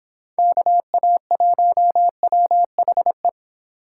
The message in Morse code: KA1WHE

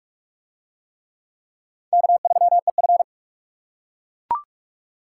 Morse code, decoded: K3EF

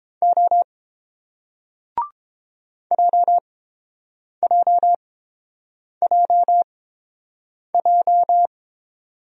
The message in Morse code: O JJJJ

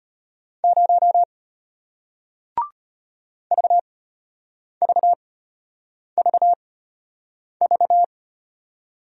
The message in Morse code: 0 VVVV

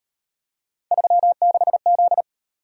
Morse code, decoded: 36Z

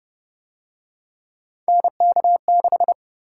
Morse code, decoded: NK6

